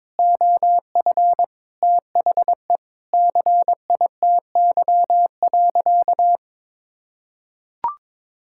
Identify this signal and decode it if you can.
Morse code, heard as a keyed tone, about 21 wpm, 700 Hz. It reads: OF THE CITY.
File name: practice / 22wpm / Sentences with 4-Word Phrases - 22wpm